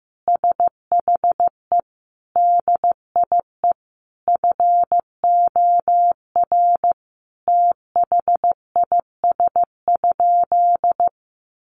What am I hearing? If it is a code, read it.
Morse code, heard as a keyed tone, about 15 wpm, 700 Hz. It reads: SHE DIE FOR THIS?
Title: Sentences from Top 1300 Words 15wpm